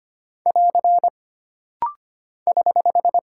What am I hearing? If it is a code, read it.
Morse code, heard as a keyed tone, about 25 wpm, 700 Hz. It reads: <AR> <HH>
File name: Single Punctuation-Prosign - Rapid-Fire - 25wpm